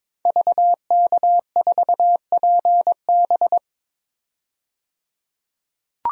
Morse code, heard as a keyed tone, about 22 wpm, 700 Hz. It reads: VK4PB
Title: DXCC-100 Call Signs - 5 Characters - 22wpm